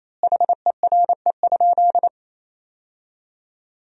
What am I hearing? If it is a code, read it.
Morse code, heard as a keyed tone, about 28 wpm, 700 Hz. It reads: HERE?
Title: 3x Word-Spacing - Sentences from Top 200 Words - 28wpm